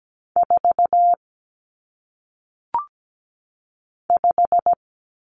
Morse code, heard as a keyed tone, about 17 wpm, 700 Hz. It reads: 4 5
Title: Numbers - 1-Digit 17wpm